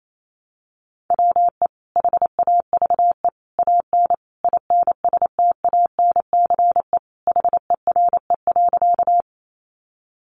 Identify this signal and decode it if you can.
Morse code, heard as a keyed tone, about 28 wpm, 700 Hz. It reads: WE HAVE AN INSTANCE HERE.
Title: Sentences from Top 800 Words 28wpm